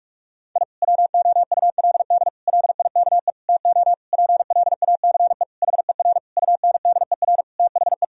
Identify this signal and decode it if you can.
Morse code, heard as a keyed tone, about 46 wpm, 700 Hz. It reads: I WOULD LIKE TO PLACE HER UNDER THE
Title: Sentences from Top 1000 Words - Review Part 1 - 45wpm